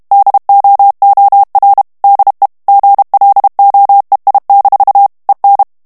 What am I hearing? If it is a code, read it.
Morse code, heard as a keyed tone, about 32 wpm, 800 Hz. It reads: DOOR DE GLOEI- EN
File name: HOLLANDAIS texte clair 2 vitesse 30 mots minute